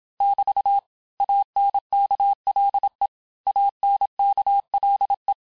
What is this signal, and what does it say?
Morse code, heard as a keyed tone, about 27 wpm, 800 Hz. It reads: <BT> ANKLE ANKLE
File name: cours-du-mercredi / audios / cours-869-texte-LES-MOTS-vitesse-25-MPM